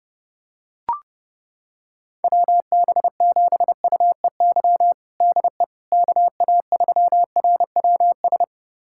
Morse code, heard as a keyed tone, about 31 wpm, 700 Hz. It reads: WB7UEY DE KA3RWS
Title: USA Callsign DE Callsign - 6 Characters - 30wpm